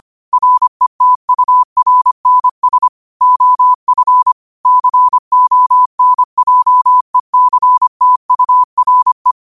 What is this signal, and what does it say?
Morse code, heard as a keyed tone, about 25 wpm, 1000 Hz. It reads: RETURNS OF CONJECTURE